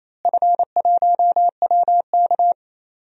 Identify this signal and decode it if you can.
Morse code, heard as a keyed tone, about 28 wpm, 700 Hz. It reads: F1WK
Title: DXCC-100 Call Signs - 4 Characters - Repeated 3x - 28wpm